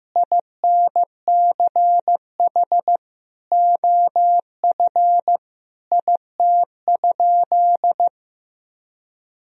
Morse code, heard as a keyed tone, about 15 wpm, 700 Hz. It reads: INCH OF IT?
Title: Sentences from Top 1000 Words - Review Part 3 - 15wpm